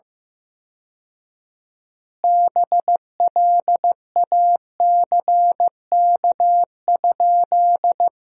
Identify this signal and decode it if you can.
Morse code, heard as a keyed tone, about 15 wpm, 700 Hz. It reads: BLACK?
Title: 4x Word-Spacing - Sentences from Top 400 Words - 15wpm